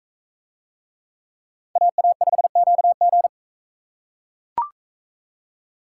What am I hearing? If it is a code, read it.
Morse code, heard as a keyed tone, about 42 wpm, 700 Hz. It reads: AA5QG